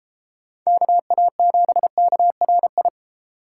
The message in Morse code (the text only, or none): KA7KRI